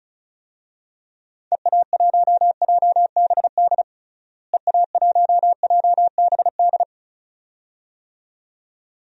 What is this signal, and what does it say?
Morse code, heard as a keyed tone, about 36 wpm, 700 Hz. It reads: EA1JBD EA1JBD
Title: DXCC-100 Call Signs - 6 Characters - Repeated 2x - 35wpm